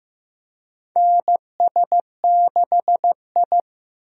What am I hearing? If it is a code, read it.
Morse code, heard as a keyed tone, about 15 wpm, 700 Hz. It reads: NS6I
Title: USA Call Signs - 4 Characters - 15wpm